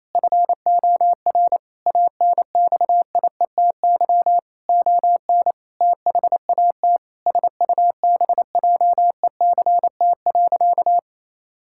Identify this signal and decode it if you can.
Morse code, heard as a keyed tone, about 27 wpm, 700 Hz. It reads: FOR ANXIETY ON THAT SUBJECT.
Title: Sentences from Top 1600 Words 28wpm